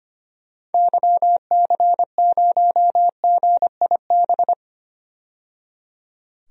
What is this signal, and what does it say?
Morse code, heard as a keyed tone, about 26 wpm, 690 Hz. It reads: YC0GIB
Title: DXCC-100 Call Signs - 6 Characters - 25wpm